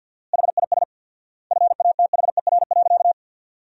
Morse code, heard as a keyed tone, about 52 wpm, 700 Hz. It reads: HIS FATHER.
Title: 4x Word-Spacing - Sentences from Top 500 Words - 50wpm